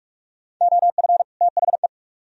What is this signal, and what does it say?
Morse code, heard as a keyed tone, about 55 wpm, 700 Hz. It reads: OF THE
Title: Speed-Racing - Sets of 2 Words 45wpm